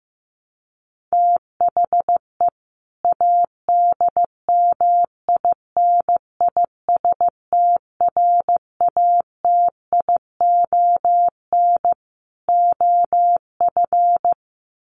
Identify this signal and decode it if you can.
Morse code, heard as a keyed tone, about 15 wpm, 700 Hz. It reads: THE ADMINISTRATION OF
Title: Sentences from Top 1300 Words 15wpm